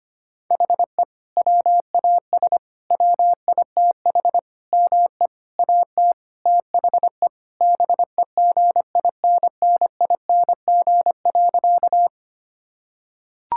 Morse code, heard as a keyed tone, about 25 wpm, 700 Hz. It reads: HE WAS WITH ME AT THE BEGINNING.
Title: Sentences from Top 1000 Words - Review Part 5 - 25wpm